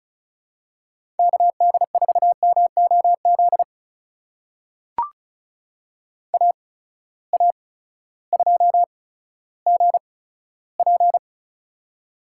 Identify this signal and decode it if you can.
Morse code, heard as a keyed tone, about 36 wpm, spaced as for 16 wpm, 700 Hz. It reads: KD4MOZ AA2GP
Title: USA Call Signs - 8x Character Spacing - 35wpm